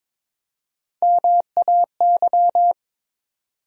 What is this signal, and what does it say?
Morse code, heard as a keyed tone, about 21 wpm, 700 Hz. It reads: MAY
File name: Build A Phrase - 2-Word Phrases - 22wpm